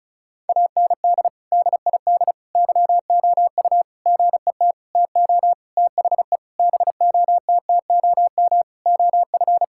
Morse code, heard as a keyed tone, about 36 wpm, 700 Hz. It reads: AND DID YOU GET TO THE BOTTOM OF
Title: Sentences from Top 1000 Words - Review Part 6 - 35wpm